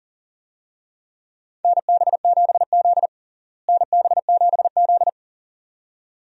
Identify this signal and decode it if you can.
Morse code, heard as a keyed tone, about 40 wpm, 700 Hz. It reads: NB7Z NB7Z
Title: USA Call Signs - 4 Characters - Repeated 2x - 40wpm